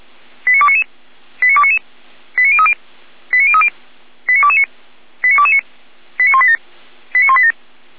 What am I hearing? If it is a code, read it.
Five-tone selective call (ZVEI), heard as quick runs of five tones, tones of about 80 ms. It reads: 89299 89299 89030 89030 89229 89229 89187 89187